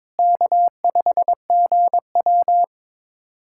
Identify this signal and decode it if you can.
Morse code, heard as a keyed tone, about 22 wpm, 700 Hz. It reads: K5GW
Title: USA Callsign DE Callsign - 4 Characters - 22wpm